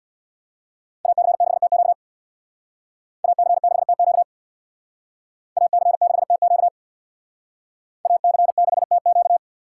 Morse code, heard as a keyed tone, about 64 wpm, 700 Hz. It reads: AC6TQ AC6TQ AC6TQ AC6TQ